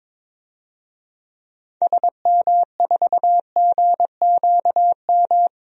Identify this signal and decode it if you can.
Morse code, heard as a keyed tone, about 22 wpm, 700 Hz. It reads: SM4GQM
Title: DXCC-100 Call Signs - 6 Characters - 22wpm